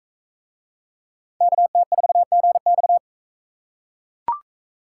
Morse code, heard as a keyed tone, about 44 wpm, 700 Hz. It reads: KT4GX